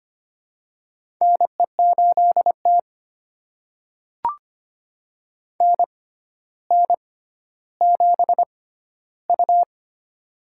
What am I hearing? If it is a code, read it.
Morse code, heard as a keyed tone, about 25 wpm, spaced as for 13 wpm, 700 Hz. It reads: NE8T NN7U
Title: USA Call Signs - 4 Characters - 6x Character Spacing - 25wpm